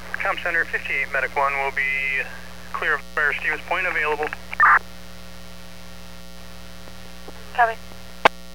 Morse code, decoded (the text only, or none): none